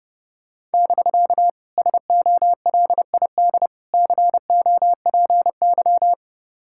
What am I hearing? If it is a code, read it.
Morse code, heard as a keyed tone, about 30 wpm, 700 Hz. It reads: <BK> SOLID COPY